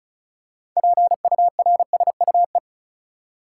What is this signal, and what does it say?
Morse code, heard as a keyed tone, about 34 wpm, 700 Hz. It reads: PURSUE
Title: Six Letter Words - 35wpm